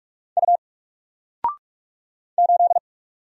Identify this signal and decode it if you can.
Morse code, heard as a keyed tone, about 44 wpm, 700 Hz. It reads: U 8
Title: Single Character (Letter:Number:Punctuation:Prosign) 45wpm